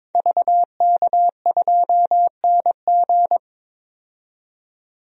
Morse code, heard as a keyed tone, about 22 wpm, 700 Hz. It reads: VK2NG
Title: DXCC-100 Call Signs - 5 Characters - 22wpm